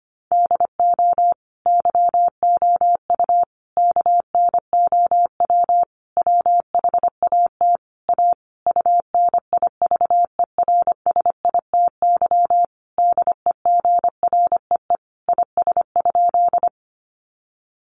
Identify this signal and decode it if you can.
Morse code, heard as a keyed tone, about 25 wpm, 700 Hz. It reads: DO YOU KNOW WHAT A UNIVERSITY DEGREE IS?
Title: Sentences from Top 1500 Words 25wpm